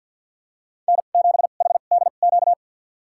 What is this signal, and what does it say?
Morse code, heard as a keyed tone, about 52 wpm, spaced as for 36 wpm, 700 Hz. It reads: N7HDQ
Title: USA Call Signs - 2x Character Spacing - 50wpm